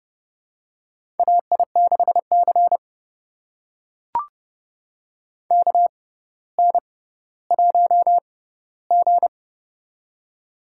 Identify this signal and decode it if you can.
Morse code, heard as a keyed tone, about 30 wpm, spaced as for 15 wpm, 700 Hz. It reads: AI6C KN1G
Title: USA Call Signs - 4 Characters - 6x Character Spacing - 30wpm